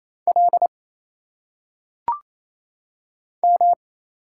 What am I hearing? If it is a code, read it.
Morse code, heard as a keyed tone, about 27 wpm, 700 Hz. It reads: L M
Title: Single Letters - Phonetic Pronunciation 28wpm